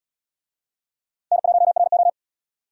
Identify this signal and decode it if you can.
Morse code, heard as a keyed tone, about 72 wpm, 700 Hz. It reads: N1RO